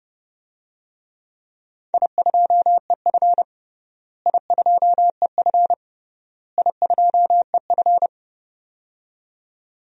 Morse code, heard as a keyed tone, about 30 wpm, 700 Hz. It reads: I2EF I2EF I2EF
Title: DXCC-100 Call Signs - 4 Characters - Repeated 3x - 30wpm